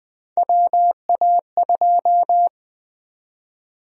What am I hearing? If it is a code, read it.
Morse code, heard as a keyed tone, about 21 wpm, 700 Hz. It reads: WA2
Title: USA Call Sign Prefixes - 20wpm